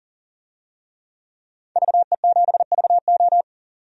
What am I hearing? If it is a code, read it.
Morse code, heard as a keyed tone, about 40 wpm, 700 Hz. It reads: VE7VO
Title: DXCC-100 Call Signs - 5 Characters - 40wpm